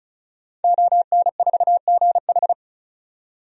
Morse code, heard as a keyed tone, about 34 wpm, 700 Hz. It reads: ON4GH
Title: DXCC-100 Call Signs - 5 Characters - 35wpm